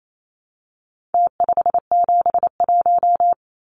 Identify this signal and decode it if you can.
Morse code, heard as a keyed tone, about 28 wpm, 700 Hz. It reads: T571
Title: Numbers - Contest Style - 4-Digit 28wpm